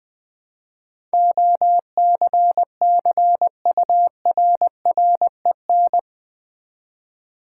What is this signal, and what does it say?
Morse code, heard as a keyed tone, about 20 wpm, 700 Hz. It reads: OCCURREN
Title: Build A Word - 4 Letters At A Time - 10-Letter Words - 20wpm